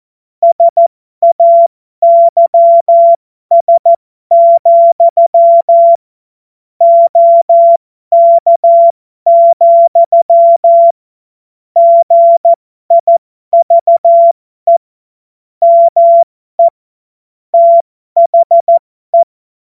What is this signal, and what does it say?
Morse code, heard as a keyed tone, about 14 wpm, spaced as for 12 wpm, 680 Hz. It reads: SAYS, OK, GIVE ME THE
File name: Morsekurs / Witze